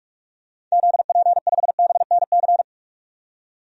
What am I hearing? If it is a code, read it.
Morse code, heard as a keyed tone, about 46 wpm, 700 Hz. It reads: ZP5BNC
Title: DXCC-100 Call Signs - 6 Characters - 45wpm